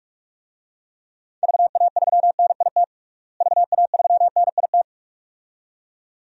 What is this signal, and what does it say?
Morse code, heard as a keyed tone, about 41 wpm, 700 Hz. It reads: VA3NIT VA3NIT
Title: DXCC-100 Call Signs - 6 Characters - Repeated 2x - 45wpm